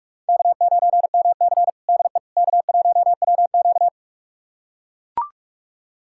Morse code, heard as a keyed tone, about 42 wpm, 700 Hz. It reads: K9MC DE K1WQ
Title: USA Callsign DE Callsign - 4 Characters - 45wpm